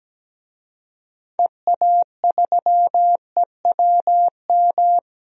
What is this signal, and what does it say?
Morse code, heard as a keyed tone, about 17 wpm, 700 Hz. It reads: EA3EWM
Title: DXCC-100 Call Signs - 6 Characters - Repeated 3x - 17wpm